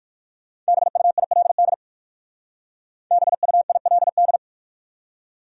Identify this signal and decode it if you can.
Morse code, heard as a keyed tone, about 60 wpm, 700 Hz. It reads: BUILD BUILD